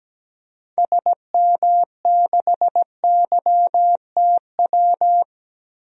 Morse code, heard as a keyed tone, about 17 wpm, 700 Hz. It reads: SM6YTW
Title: DXCC-100 Call Signs - 6 Characters - Repeated 2x - 17wpm